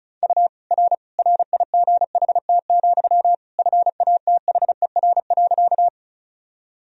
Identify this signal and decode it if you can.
Morse code, heard as a keyed tone, about 34 wpm, 700 Hz. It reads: U R RIGHT, FATHER.
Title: Sentences from Top 300 Words and Abbreviations 35wpm